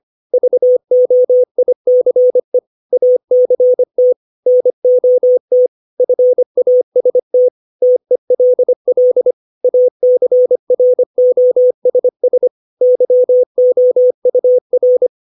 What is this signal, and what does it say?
Morse code, heard as a keyed tone, about 25 wpm, 500 Hz. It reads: VOICE ACT NOT FAST TELL ACROSS YOUR
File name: 632 words / 632 Words 25WPM